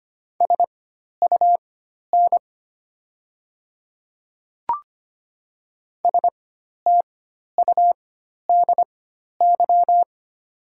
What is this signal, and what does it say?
Morse code, heard as a keyed tone, about 25 wpm, spaced as for 12 wpm, 690 Hz. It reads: SUN STUDY
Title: 4x Character Spacing - Top 300 Words - 25wpm